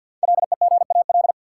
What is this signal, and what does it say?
Morse code, heard as a keyed tone, about 52 wpm, 700 Hz. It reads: LEGAL
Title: Sentences from Top 1200 Words - New Word Drill Alt - 50wpm